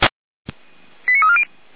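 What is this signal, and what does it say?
Five-tone selective call (ZVEI), heard as quick runs of five tones, tones of about 80 ms. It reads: 89340
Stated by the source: ZVEI five-tone selective call received by SDR.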